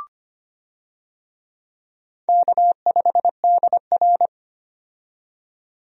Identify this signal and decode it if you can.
Morse code, heard as a keyed tone, about 25 wpm, 700 Hz. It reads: K5DR
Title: USA Call Signs - 4 Characters - 25wpm